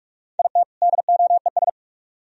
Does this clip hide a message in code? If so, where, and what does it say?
Yes: Morse code from 0.4 to 1.7 s, heard as a keyed tone, about 44 wpm, 700 Hz. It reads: IT DOES